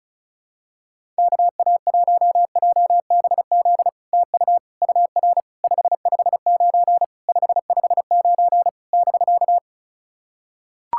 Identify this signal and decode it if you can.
Morse code, heard as a keyed tone, about 36 wpm, 700 Hz. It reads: KA1JBZ TU UR 559 559 <BK>
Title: SOTA QSO Practice 35wpm